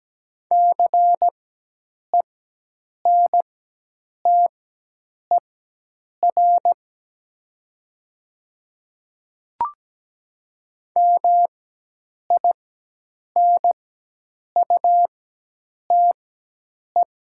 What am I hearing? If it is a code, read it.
Morse code, heard as a keyed tone, about 17 wpm, spaced as for 8 wpm, 700 Hz. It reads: CENTER MINUTE
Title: 4x Character Spacing - Top 500 Words - 17wpm